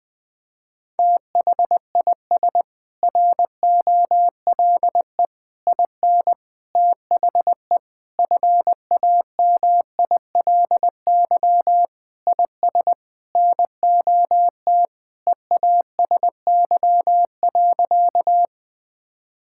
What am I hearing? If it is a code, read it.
Morse code, heard as a keyed tone, about 21 wpm, 700 Hz. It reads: THIS ROLE IN THE FAMILY IS NOT EASY.
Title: Sentences from Top 1000 Words - Review Part 6 - 20wpm